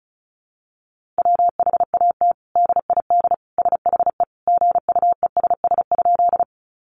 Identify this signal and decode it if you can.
Morse code, heard as a keyed tone, about 34 wpm, 700 Hz. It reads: WHAT DID SHE GUESS?
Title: Sentences from Top 1500 Words 35wpm